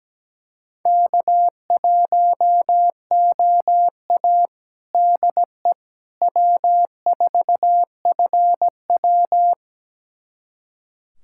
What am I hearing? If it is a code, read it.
Morse code, heard as a keyed tone, about 17 wpm, 700 Hz. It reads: K1OA DE W4FW
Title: USA Callsign DE Callsign - 4 Characters - 17wpm